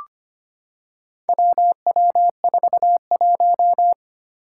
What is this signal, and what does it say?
Morse code, heard as a keyed tone, about 25 wpm, 700 Hz. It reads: WW41
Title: Four Challenging Letter-Number Combinations 25wpm